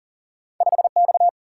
Morse code, heard as a keyed tone, about 40 wpm, 700 Hz. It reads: 5X
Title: Single-Letter Single-Number 40wpm